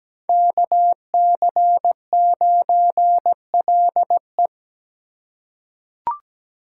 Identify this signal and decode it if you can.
Morse code, heard as a keyed tone, about 17 wpm, 700 Hz. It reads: KC9LE
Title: USA Call Signs - 5 Characters - 4x Character Spacing - 17wpm